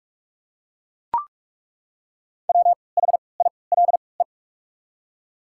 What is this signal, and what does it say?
Morse code, heard as a keyed tone, about 42 wpm, spaced as for 25 wpm, 700 Hz. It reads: WHILE
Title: 3x Character Spacing - Top 300 Words - 45wpm